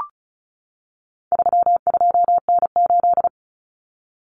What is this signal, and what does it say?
Morse code, heard as a keyed tone, about 34 wpm, 700 Hz. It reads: 32N8
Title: Common Cut Numbers - 4-Digit 35wpm